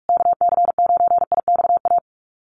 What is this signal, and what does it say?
Morse code, heard as a keyed tone, about 44 wpm, 700 Hz. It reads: KC9IXA
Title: USA Call Signs Repeated 3x 45wpm